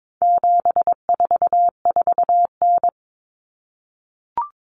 Morse code, heard as a keyed tone, about 22 wpm, 700 Hz. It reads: 744N
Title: Common Cut Numbers - 4-Digit 22wpm